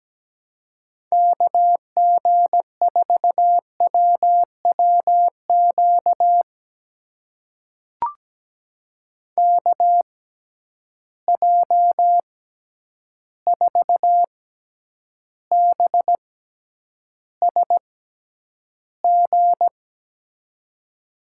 Morse code, heard as a keyed tone, about 17 wpm, spaced as for 9 wpm, 700 Hz. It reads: KG4WWQ KJ4BSG
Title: USA Call Signs - 6 Characters - 6x Character Spacing - 17wpm